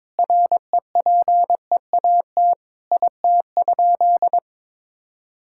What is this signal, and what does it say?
Morse code, heard as a keyed tone, about 22 wpm, 700 Hz. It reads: REPEAT IT?
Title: Sentences from Top 1600 Words 22wpm